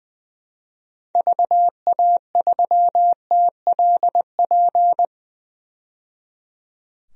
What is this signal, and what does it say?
Morse code, heard as a keyed tone, about 20 wpm, 700 Hz. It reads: VA3TLP
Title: DXCC-100 Call Signs - 6 Characters - 20wpm